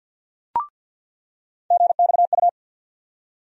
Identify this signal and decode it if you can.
Morse code, heard as a keyed tone, about 47 wpm, 700 Hz. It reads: GXU
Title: USA Call Sign Postfixes - 3 Letters 50wpm